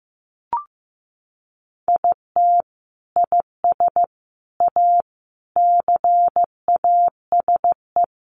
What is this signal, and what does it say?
Morse code, heard as a keyed tone, about 15 wpm, 700 Hz. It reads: IT IS A CASE
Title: Sentences from Top 1400 Words 15wpm